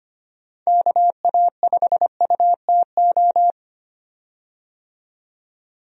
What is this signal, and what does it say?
Morse code, heard as a keyed tone, about 25 wpm, 700 Hz. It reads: KA5UTO
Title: USA Call Signs - 6 Characters - 25wpm